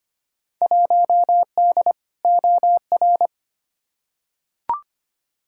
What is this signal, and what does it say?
Morse code, heard as a keyed tone, about 25 wpm, 700 Hz. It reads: 1D OR
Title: Field Day Exchanges - 25wpm